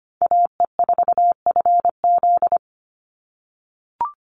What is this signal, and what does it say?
Morse code, heard as a keyed tone, about 25 wpm, 700 Hz. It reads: AE4FZ
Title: NA SOTA Chaser Callsigns 25wpm